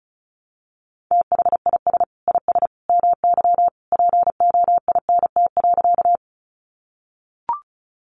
Morse code, heard as a keyed tone, about 34 wpm, 700 Hz. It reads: THIS IS MY POINT.
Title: Sentences from Top 200 Words 35wpm